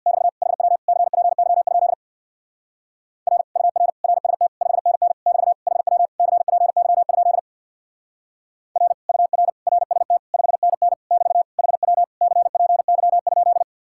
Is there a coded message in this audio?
Yes: Morse code from 0.1 to 13.6 s, heard as a keyed tone, about 65 wpm, 700 Hz. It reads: <BT> HW CPY? R UR RST 5NN <BT> HW CPY? R UR RST 5NN <BT> HW CPY?